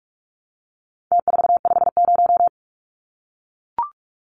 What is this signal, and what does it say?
Morse code, heard as a keyed tone, about 44 wpm, 700 Hz. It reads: T450
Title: Numbers - Contest Style - 4-Digit 45wpm